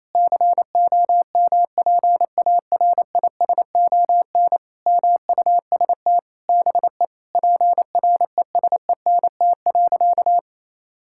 Morse code, heard as a keyed tone, about 28 wpm, 700 Hz. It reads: COMPARISON MUST BE PRESENT.
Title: Sentences from Top 1600 Words 28wpm